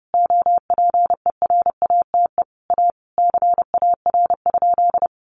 Morse code, heard as a keyed tone, about 30 wpm, 700 Hz. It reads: OPERATE A CAR?
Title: Sentences from Top 1500 Words 30wpm